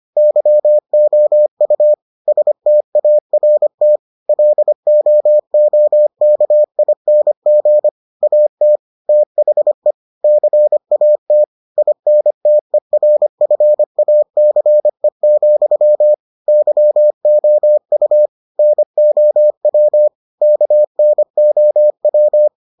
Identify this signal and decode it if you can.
Morse code, heard as a keyed tone, about 25 wpm, 600 Hz. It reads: YOU START LOOKING AT THE CAT INTERFACE, YOU NOW KNOW